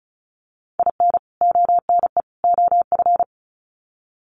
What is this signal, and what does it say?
Morse code, heard as a keyed tone, about 34 wpm, 700 Hz. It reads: IN ONE OF
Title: Sets of 3 Words 35wpm